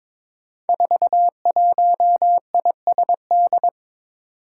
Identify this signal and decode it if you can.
Morse code, heard as a keyed tone, about 22 wpm, 700 Hz. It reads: 41ISD
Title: Five Challenging Letter-Number Combinations 22wpm